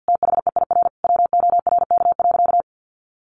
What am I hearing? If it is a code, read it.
Morse code, heard as a keyed tone, about 50 wpm, 700 Hz. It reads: THEIR WORK.
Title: Sentences from Top 1100 Words 50wpm